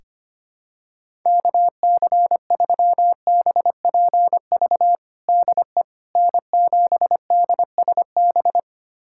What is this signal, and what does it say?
Morse code, heard as a keyed tone, about 25 wpm, 700 Hz. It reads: KC3BPV DE N7DSB
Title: USA Callsign DE Callsign 25wpm